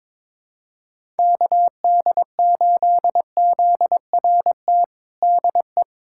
Morse code, heard as a keyed tone, about 22 wpm, 700 Hz. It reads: KD8ZRT DE
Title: USA Callsign DE Callsign 22wpm